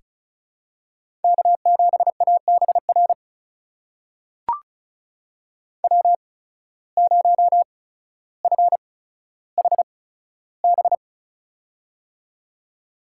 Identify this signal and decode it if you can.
Morse code, heard as a keyed tone, about 34 wpm, spaced as for 15 wpm, 700 Hz. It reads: K7ABR W0FHB
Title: USA Call Signs - 5 Characters - 8x Character Spacing - 35wpm